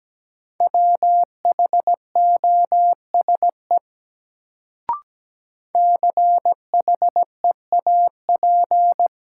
Morse code, heard as a keyed tone, about 17 wpm, 700 Hz. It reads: WHOSE CHEAP
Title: Sentences from Top 1200 Words - New Word Drill Alt - 17wpm